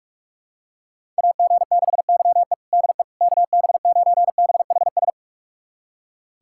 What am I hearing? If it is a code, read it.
Morse code, heard as a keyed tone, about 46 wpm, 690 Hz. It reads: AG6YE DE KB9BHS
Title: USA Callsign DE Callsign 45wpm